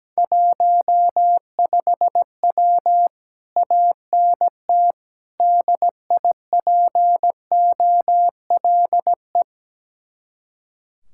Morse code, heard as a keyed tone, about 17 wpm, 700 Hz. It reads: 15W ANT DIPOLE
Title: Speed-Racing Common QSO Phrases - 17wpm